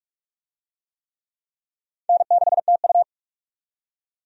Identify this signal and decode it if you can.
Morse code, heard as a keyed tone, about 46 wpm, 700 Hz. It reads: N6TU
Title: USA Call Signs - 4 Characters - 45wpm